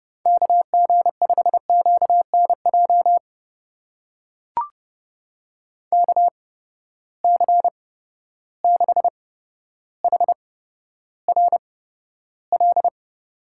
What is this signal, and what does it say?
Morse code, heard as a keyed tone, about 30 wpm, spaced as for 13 wpm, 700 Hz. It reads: KG5QNJ KC6HRL
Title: USA Call Signs - 6 Characters - 8x Character Spacing - 30wpm